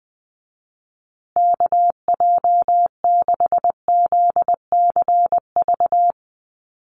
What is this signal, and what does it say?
Morse code, heard as a keyed tone, about 20 wpm, 700 Hz. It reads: KJ6ZCV
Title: USA Call Signs Repeated 3x 20wpm